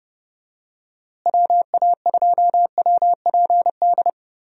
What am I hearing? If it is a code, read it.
Morse code, heard as a keyed tone, about 31 wpm, 700 Hz. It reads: WA2WPD
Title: USA Call Signs - 6 Characters - Repeated 2x - 30wpm